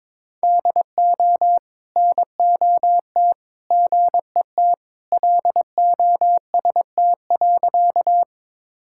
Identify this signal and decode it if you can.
Morse code, heard as a keyed tone, about 22 wpm, 700 Hz. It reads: DO NOT GET LOST.
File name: practice / 22wpm / Sentences with 3-Word Phrases - 22wpm